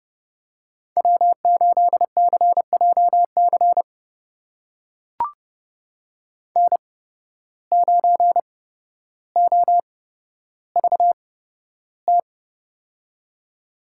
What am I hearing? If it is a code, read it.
Morse code, heard as a keyed tone, about 31 wpm, spaced as for 13 wpm, 700 Hz. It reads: W8CJC N9OVT
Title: USA Call Signs - 5 Characters - 8x Character Spacing - 30wpm